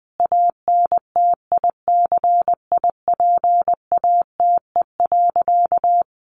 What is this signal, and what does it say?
Morse code, heard as a keyed tone, about 20 wpm, 700 Hz. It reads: ANTICIPATE.
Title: Sentences from Top 1500 Words 20wpm